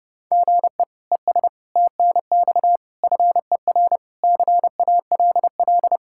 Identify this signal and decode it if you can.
Morse code, heard as a keyed tone, about 30 wpm, 700 Hz. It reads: GE ES TNX FER CALL
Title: K1USN SST QSO Practice 30wpm